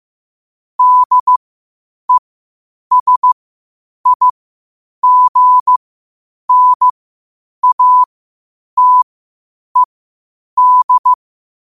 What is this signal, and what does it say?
Morse code, heard as a keyed tone, about 15 wpm, spaced as for 8 wpm, 1000 Hz. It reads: DESIGNATED
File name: qotdfiles / 8wpm